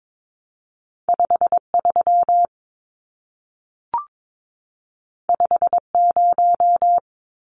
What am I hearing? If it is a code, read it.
Morse code, heard as a keyed tone, about 22 wpm, 700 Hz. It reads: 53 50
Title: Numbers - 2-Digit 22wpm